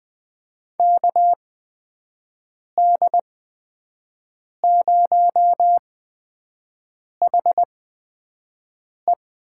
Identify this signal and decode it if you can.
Morse code, heard as a keyed tone, about 20 wpm, spaced as for 5 wpm, 700 Hz. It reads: KD0HE
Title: USA Call Signs - 6 Characters - 8x Character Spacing - 20wpm